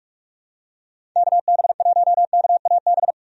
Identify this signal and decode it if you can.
Morse code, heard as a keyed tone, about 46 wpm, 700 Hz. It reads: KB1KAB
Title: USA Call Signs - 6 Characters - 45wpm